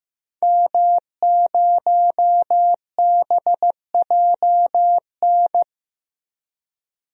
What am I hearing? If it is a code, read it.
Morse code, heard as a keyed tone, about 15 wpm, 700 Hz. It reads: M0BJN
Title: DXCC-100 Call Signs - 5 Characters - Repeated 3x - 15wpm